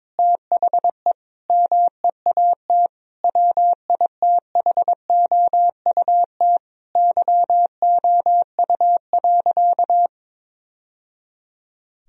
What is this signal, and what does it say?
Morse code, heard as a keyed tone, about 22 wpm, 700 Hz. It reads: THE MEAT WITHOUT YOU.
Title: Sentences from Top 1000 Words - Review Part 3 - 22wpm